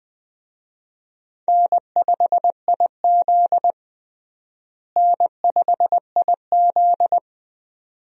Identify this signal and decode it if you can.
Morse code, heard as a keyed tone, about 20 wpm, 700 Hz. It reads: N5IZ N5IZ
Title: USA Call Signs - 4 Characters - Repeated 2x - 20wpm